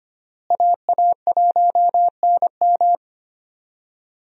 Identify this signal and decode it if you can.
Morse code, heard as a keyed tone, about 25 wpm, 700 Hz. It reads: AA1NM
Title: USA Call Signs - 5 Characters - Repeated 3x - 25wpm